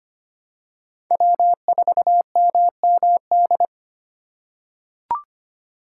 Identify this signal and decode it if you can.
Morse code, heard as a keyed tone, about 25 wpm, 700 Hz. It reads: W4MMD
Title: K1USN SST QSO Practice - 25wpm